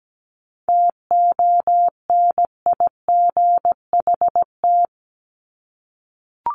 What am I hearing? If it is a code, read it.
Morse code, heard as a keyed tone, about 17 wpm, 700 Hz. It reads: TONIGHT
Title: Top 501-750 Words 17wpm